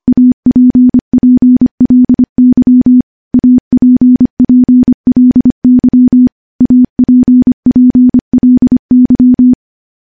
Morse code, heard as a keyed tone, about 25 wpm, 250 Hz. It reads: APPLY APPLY APPLY
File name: morse / morse-training-25WPM